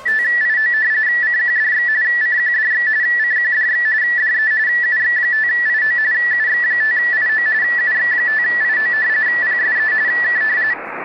RTTY, about 45.45 baud. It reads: TIO CQDX CQDX CQDX DE UX5IO UX5IO UX5IO PSE K